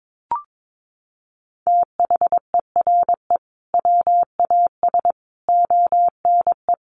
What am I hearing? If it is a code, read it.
Morse code, heard as a keyed tone, about 22 wpm, 700 Hz. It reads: THERE WAS ONE
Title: Sentences from Top 1500 Words 22wpm